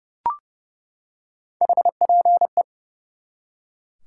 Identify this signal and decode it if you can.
Morse code, heard as a keyed tone, about 30 wpm, 700 Hz. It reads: HPE
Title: Top 90 QSO Elements 30wpm